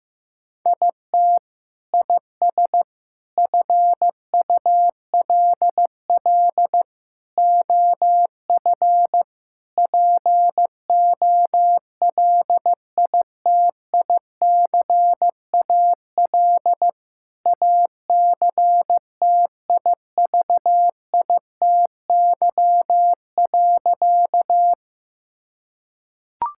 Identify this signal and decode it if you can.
Morse code, heard as a keyed tone, about 15 wpm, 700 Hz. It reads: IT IS FULL OF POLITICAL ACTIVITY.
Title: Sentences from Top 1000 Words - Review Part 5 - 15wpm